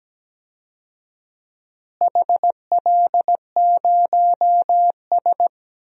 Morse code, heard as a keyed tone, about 17 wpm, 700 Hz. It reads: HL0S
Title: DXCC-100 Call Signs - 4 Characters - Repeated 2x - 17wpm